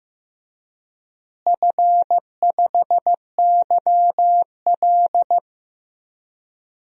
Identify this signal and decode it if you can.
Morse code, heard as a keyed tone, about 15 wpm, 700 Hz. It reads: F5YL
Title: DXCC-100 Call Signs - 4 Characters - Repeated 2x - 15wpm